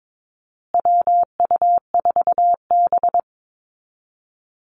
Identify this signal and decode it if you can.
Morse code, heard as a keyed tone, about 22 wpm, 700 Hz. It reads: WU4B
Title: Call Signs Repeated 2x - 22wpm